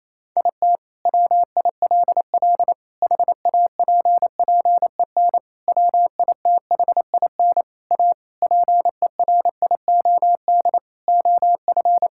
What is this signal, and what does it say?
Morse code, heard as a keyed tone, about 28 wpm, 700 Hz. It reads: IT WILL HAPPEN WITHIN A PERIOD OF
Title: Sentences from Top 1000 Words - Review Part 5 - 28wpm